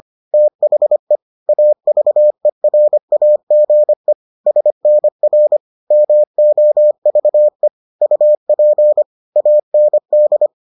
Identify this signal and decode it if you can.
Morse code, heard as a keyed tone, about 25 wpm, 600 Hz. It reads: THE AVERAGE SNR MOVE UP AND